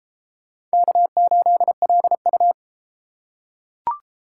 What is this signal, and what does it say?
Morse code, heard as a keyed tone, about 30 wpm, 700 Hz. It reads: K8LU